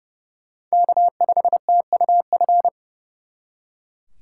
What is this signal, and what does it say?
Morse code, heard as a keyed tone, about 38 wpm, 700 Hz. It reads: K5TUF